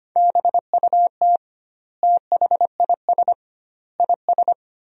Morse code, heard as a keyed tone, about 25 wpm, 700 Hz. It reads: BUT THIS IS
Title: 2x Word-Spacing - Sentences from Top 200 Words - 25wpm